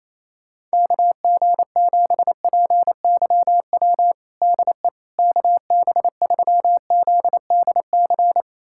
Morse code, heard as a keyed tone, about 28 wpm, 700 Hz. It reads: KG7PYW DE KB3ZDC
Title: USA Callsign DE Callsign - 6 Characters - 28wpm